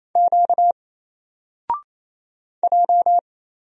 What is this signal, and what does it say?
Morse code, heard as a keyed tone, about 27 wpm, 700 Hz. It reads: Q J
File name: Single-Letter-Number-28wpm